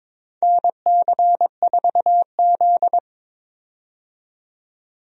Morse code, heard as a keyed tone, about 22 wpm, 700 Hz. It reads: NC4Z
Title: USA Call Signs - 4 Characters - 22wpm